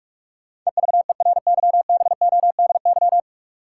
Morse code, heard as a keyed tone, about 46 wpm, 700 Hz. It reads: EVERYBODY
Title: Sentences from Top 1400 Words - New Word Drill - 45wpm